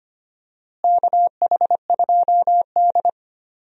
Morse code, heard as a keyed tone, about 25 wpm, 700 Hz. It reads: KH2D
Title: USA Call Signs - 4 Characters - Repeated 2x - 25wpm